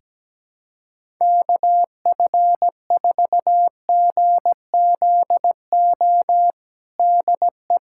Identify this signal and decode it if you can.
Morse code, heard as a keyed tone, about 17 wpm, 700 Hz. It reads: KF4GZO DE
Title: USA Callsign DE Callsign - 6 Characters - 17wpm